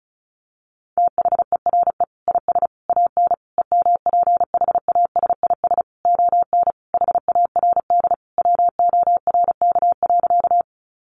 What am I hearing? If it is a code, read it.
Morse code, heard as a keyed tone, about 34 wpm, 700 Hz. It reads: THERE IS AN EMPHASIS ON HARD WORK.
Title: Sentences from Top 1400 Words 35wpm